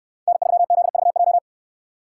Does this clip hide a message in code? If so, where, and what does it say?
Yes: Morse code from 0.3 to 1.4 s, heard as a keyed tone, about 65 wpm, 700 Hz. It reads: N3ZVJ